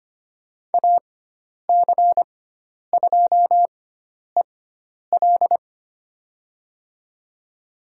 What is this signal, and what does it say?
Morse code, heard as a keyed tone, about 25 wpm, spaced as for 10 wpm, 700 Hz. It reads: AC2EL
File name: Lesson 16g - Letter c - Call Signs 25wpm